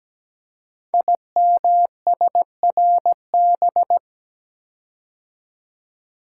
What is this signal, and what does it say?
Morse code, heard as a keyed tone, about 17 wpm, 700 Hz. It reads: IMSRB
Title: Code Groups - 5 Letters - 17wpm